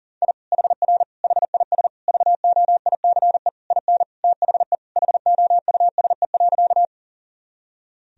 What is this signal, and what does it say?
Morse code, heard as a keyed tone, about 40 wpm, 700 Hz. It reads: I HR HIS VOICE IN THE HOUSE.
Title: Sentences from Top 500 Words and Abbreviations 40wpm